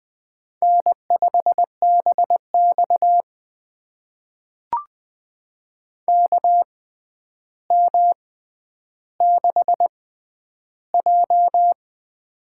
Morse code, heard as a keyed tone, about 20 wpm, spaced as for 10 wpm, 700 Hz. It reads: N5BX KM6J
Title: USA Call Signs - 4 Characters - 6x Character Spacing - 20wpm